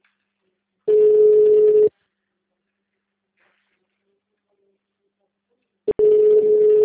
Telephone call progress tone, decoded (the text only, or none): ringback tone